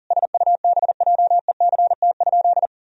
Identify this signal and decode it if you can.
Morse code, heard as a keyed tone, about 40 wpm, 700 Hz. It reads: SUBJECT?
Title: Sentences from Top 1000 Words - Review Part 4 - 40wpm